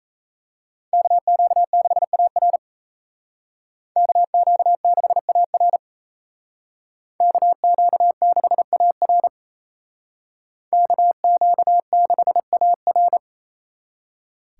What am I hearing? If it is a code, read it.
Morse code, heard as a keyed tone, about 35 wpm, 700 Hz. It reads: KQ6AR KQ6AR KQ6AR KQ6AR
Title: Speed Racing - USA Call Signs 28wpm